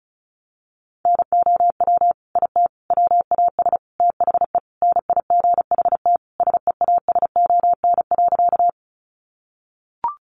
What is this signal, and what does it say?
Morse code, heard as a keyed tone, about 34 wpm, 700 Hz. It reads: NOW IT WAS THE NIGHT SEASON.
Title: Sentences from Top 1400 Words 35wpm